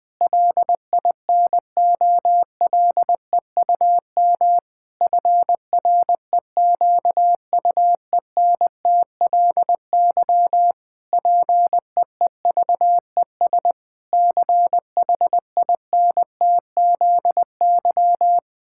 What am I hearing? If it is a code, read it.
Morse code, heard as a keyed tone, about 20 wpm, 700 Hz. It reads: LINOLEUM FREQUENTLY PEEVES CHINTZY
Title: Pangram Challenge v2 - 20wpm